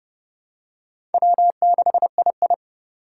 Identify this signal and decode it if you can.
Morse code, heard as a keyed tone, about 31 wpm, 700 Hz. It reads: W6II
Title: USA Call Signs - 6x Character Spacing - 30wpm